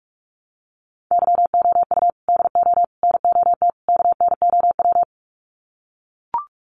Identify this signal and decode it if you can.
Morse code, heard as a keyed tone, about 44 wpm, 700 Hz. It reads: YOU DO NOT KNOW
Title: Sets of 4 Words 45wpm